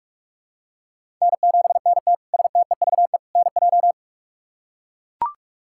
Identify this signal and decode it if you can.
Morse code, heard as a keyed tone, about 43 wpm, 700 Hz. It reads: N7NT STEVE NJ
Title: K1USN SST QSO Practice - 45wpm